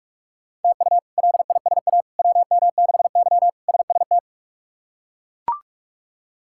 Morse code, heard as a keyed tone, about 42 wpm, 700 Hz. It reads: TU LISA WM6Y SST
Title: K1USN SST QSO Practice - 45wpm